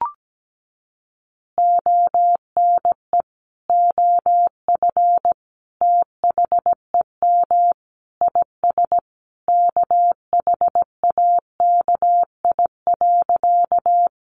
Morse code, heard as a keyed tone, about 17 wpm, 700 Hz. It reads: ONE OF THEM IS KHAKI.
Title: Additional Easy Phrases 17wpm